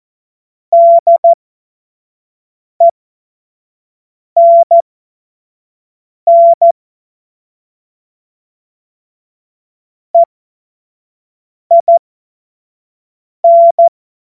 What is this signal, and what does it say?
Morse code, heard as a keyed tone, about 14 wpm, spaced as for 5 wpm, 680 Hz. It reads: DENN EIN
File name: Morsekurs / Witze / Witz 17. August 2021, 25 F70